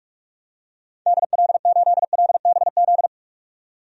Morse code, heard as a keyed tone, about 47 wpm, 700 Hz. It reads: DL8LBZ
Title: DXCC-100 Call Signs - 6 Characters - 45wpm